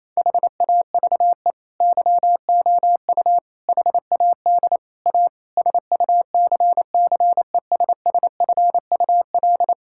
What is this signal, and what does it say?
Morse code, heard as a keyed tone, about 28 wpm, 690 Hz. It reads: HAVE YOU HAD A SUCCESSFUL
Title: Sentences from Top 1000 Words - Review Part 6 - 28wpm